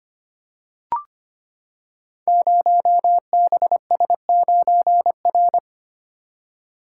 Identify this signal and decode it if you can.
Morse code, heard as a keyed tone, about 25 wpm, 700 Hz. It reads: 0BS9R
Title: Code Groups - 5 Letter:Numbers - 25wpm